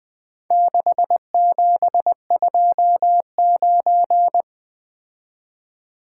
Morse code, heard as a keyed tone, about 22 wpm, 700 Hz. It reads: 6729
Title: Speed-Racing - Four Letter-Number Combinations - 20wpm